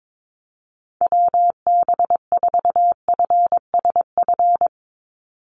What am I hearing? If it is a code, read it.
Morse code, heard as a keyed tone, about 22 wpm, 700 Hz. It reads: WB4FSF
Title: Call Signs - 22wpm